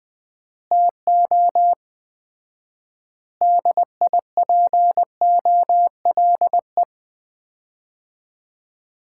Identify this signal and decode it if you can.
Morse code, heard as a keyed tone, about 20 wpm, 700 Hz. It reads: TO DIPOLE
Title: Common QSO Phrases - 4x Word-Spacing - 20wpm